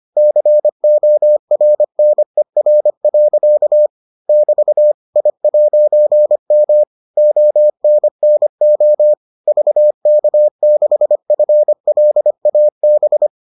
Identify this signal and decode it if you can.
Morse code, heard as a keyed tone, about 25 wpm, 600 Hz. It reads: CORNER. <BT> I'M ONNO VK6FLAB